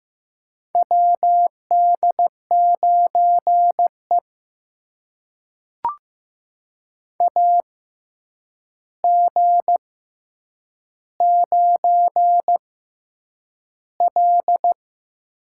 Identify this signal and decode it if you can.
Morse code, heard as a keyed tone, about 15 wpm, spaced as for 8 wpm, 700 Hz. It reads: WD9E AG9L
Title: USA Call Signs - 4 Characters - 6x Character Spacing - 15wpm